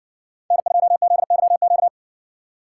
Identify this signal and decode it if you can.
Morse code, heard as a keyed tone, about 60 wpm, 700 Hz. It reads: N2ZYQ